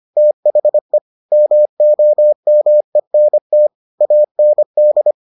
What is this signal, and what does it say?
Morse code, heard as a keyed tone, about 25 wpm, 600 Hz. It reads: THE MOMENT AND